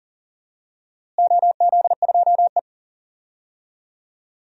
Morse code, heard as a keyed tone, about 40 wpm, 700 Hz. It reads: OZ2E
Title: DXCC-100 Call Signs - 4 Characters - Repeated 2x - 40wpm